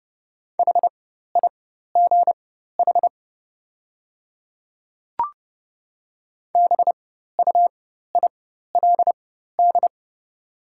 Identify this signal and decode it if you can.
Morse code, heard as a keyed tone, about 30 wpm, spaced as for 14 wpm, 700 Hz. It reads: HIGH BUILD
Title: 4x Character Spacing - Top 300 Words - 30wpm